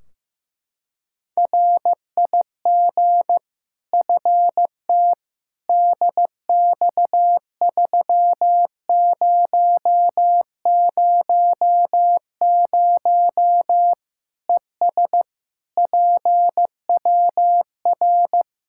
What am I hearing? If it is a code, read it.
Morse code, heard as a keyed tone, about 15 wpm, 700 Hz. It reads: RIG FT DX3000 ES PWR
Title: Practice QSO - Exchange 2 - Info - 15wpm